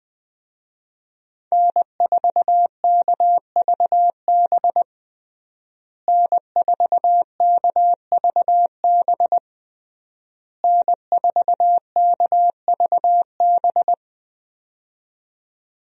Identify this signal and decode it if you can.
Morse code, heard as a keyed tone, about 21 wpm, 700 Hz. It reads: N4KVB N4KVB N4KVB
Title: USA Call Signs - 5 Characters - Repeated 3x - 20wpm